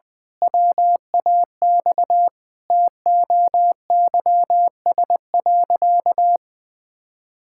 Morse code, heard as a keyed tone, about 21 wpm, 700 Hz. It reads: WAX TOYS.
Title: Pangram Challenge v2 - 20wpm